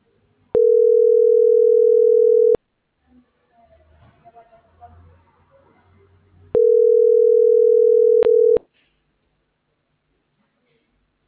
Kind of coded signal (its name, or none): telephone call progress tone